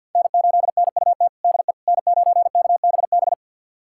Morse code, heard as a keyed tone, about 52 wpm, 700 Hz. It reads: N8NUT DE N9KBB
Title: USA Callsign DE Callsign - 5 Characters - 50wpm